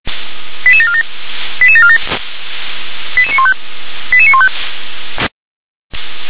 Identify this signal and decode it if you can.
Five-tone selective call (ZVEI), heard as quick runs of five tones, tones of about 70 ms. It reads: 88747 88747 88915 88915
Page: /357/SDR/ZVEI/